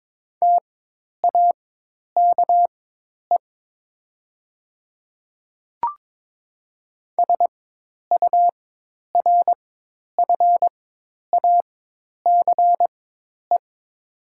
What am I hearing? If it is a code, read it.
Morse code, heard as a keyed tone, about 22 wpm, spaced as for 10 wpm, 700 Hz. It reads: TAKE SURFACE
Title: 4x Character Spacing - Top 500 Words - 22wpm